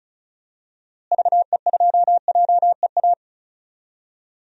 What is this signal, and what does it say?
Morse code, heard as a keyed tone, about 36 wpm, 700 Hz. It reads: VE2JEA
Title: DXCC-100 Call Signs - 6 Characters - Repeated 2x - 35wpm